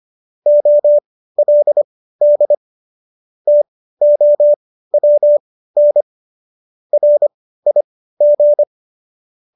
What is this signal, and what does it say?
Morse code, heard as a keyed tone, about 25 wpm, spaced as for 15 wpm, 590 Hz. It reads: OLD TOWN RIG